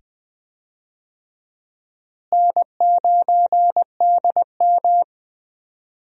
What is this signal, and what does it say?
Morse code, heard as a keyed tone, about 20 wpm, 700 Hz. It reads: N9DM
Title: USA Call Signs - 4 Characters - Repeated 2x - 20wpm